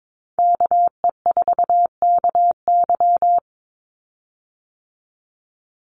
Morse code, heard as a keyed tone, about 22 wpm, 690 Hz. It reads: KE4KY
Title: Call Signs Repeated 2x - 22wpm